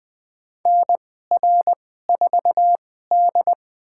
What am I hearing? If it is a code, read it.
Morse code, heard as a keyed tone, about 21 wpm, spaced as for 15 wpm, 700 Hz. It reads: NR4D
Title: USA Call Signs - 2x Character Spacing - 20wpm